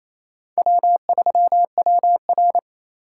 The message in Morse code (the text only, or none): W3WR